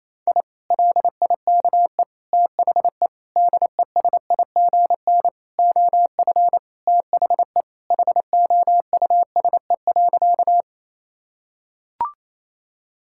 Morse code, heard as a keyed tone, about 28 wpm, 700 Hz. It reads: I LIKE THE DESIGN OF THE HOUSE.
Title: Sentences from Top 1000 Words - Review Part 5 - 28wpm